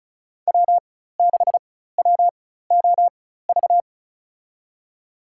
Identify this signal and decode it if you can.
Morse code, heard as a keyed tone, about 36 wpm, spaced as for 16 wpm, 700 Hz. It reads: W6WOV
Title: USA Call Signs - 5 Characters - 4x Character Spacing - 35wpm